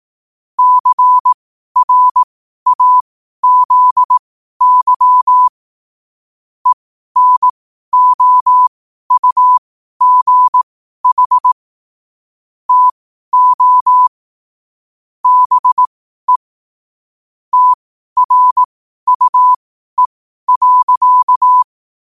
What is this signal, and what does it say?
Morse code, heard as a keyed tone, about 18 wpm, spaced as for 12 wpm, 1000 Hz. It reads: CRAZY ENOUGH TO BE TRUE.
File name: qotdfiles / 12wpm